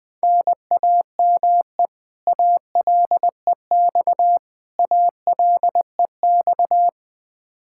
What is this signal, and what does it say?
Morse code, heard as a keyed tone, about 20 wpm, 700 Hz. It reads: NAME ALEX ALEX
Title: Practice QSO - Exchange 1 - Beginning - 20wpm